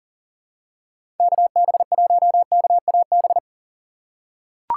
Morse code, heard as a keyed tone, about 40 wpm, 700 Hz. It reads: KB1KAB
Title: USA Call Signs - 6 Characters - 4x Character Spacing - 40wpm